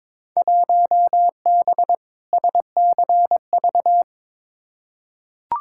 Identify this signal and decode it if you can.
Morse code, heard as a keyed tone, about 22 wpm, 700 Hz. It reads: 1B SCV
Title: Field Day Exchanges - 22wpm